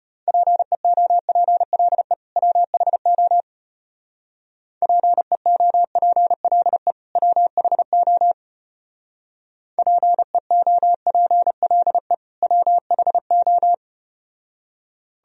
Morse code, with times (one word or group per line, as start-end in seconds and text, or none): 0.3-2.1 PEOPLE
2.4-3.4 WHO
4.8-6.9 PEOPLE
7.2-8.3 WHO
9.8-12.1 PEOPLE
12.4-13.7 WHO